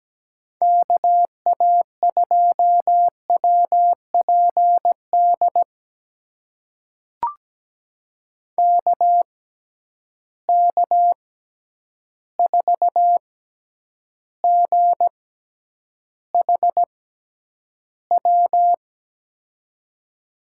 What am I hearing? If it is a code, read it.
Morse code, heard as a keyed tone, about 17 wpm, spaced as for 9 wpm, 700 Hz. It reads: KA2WPD KK4GHW
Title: USA Call Signs - 6x Character Spacing - 17wpm